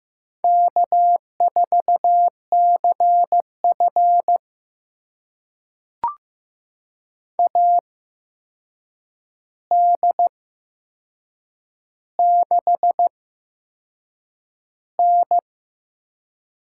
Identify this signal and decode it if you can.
Morse code, heard as a keyed tone, about 15 wpm, spaced as for 6 wpm, 700 Hz. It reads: K4CF AD6N
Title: USA Call Signs - 4 Characters - 8x Character Spacing - 15wpm